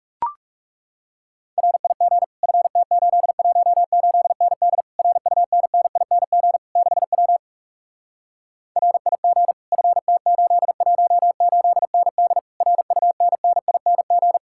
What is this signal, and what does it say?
Morse code, heard as a keyed tone, about 40 wpm, 700 Hz. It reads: RIG FT818ND RUNNING 6W RIG FT818ND RUNNING